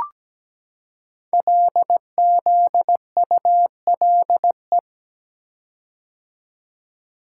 Morse code, heard as a keyed tone, about 17 wpm, 700 Hz. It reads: LZULE
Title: Code Groups - 5 Letters - 17wpm